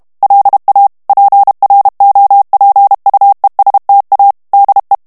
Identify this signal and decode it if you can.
Morse code, heard as a keyed tone, about 32 wpm, 800 Hz. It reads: LA PROPUESTA DE
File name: morse mp3 / ESPAGNOL TEXTE EN CLAIR 1 VITESSE 30 MOTS MINUTES